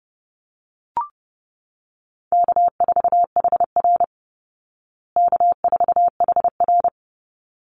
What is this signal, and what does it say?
Morse code, heard as a keyed tone, about 30 wpm, 700 Hz. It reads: K4HR K4HR
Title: Easy Call Signs Repeated 2x - 30wpm